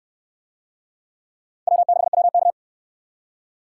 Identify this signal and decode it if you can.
Morse code, heard as a keyed tone, about 63 wpm, 700 Hz. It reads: W6WK